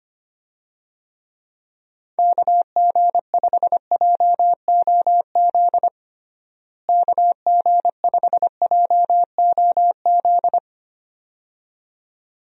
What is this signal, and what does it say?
Morse code, heard as a keyed tone, about 25 wpm, 700 Hz. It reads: KG5JOZ KG5JOZ
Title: USA Call Signs - 6 Characters - Repeated 2x - 25wpm